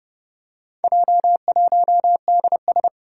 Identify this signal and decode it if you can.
Morse code, heard as a keyed tone, about 30 wpm, 700 Hz. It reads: J1DS
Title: Four Challenging Letter-Number Combinations 30wpm